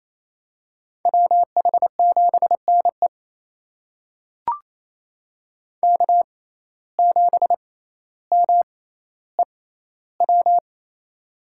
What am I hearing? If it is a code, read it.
Morse code, heard as a keyed tone, about 28 wpm, spaced as for 14 wpm, 700 Hz. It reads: WH7NE K7MEW
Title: USA Call Signs - 5 Characters - 6x Character Spacing - 28wpm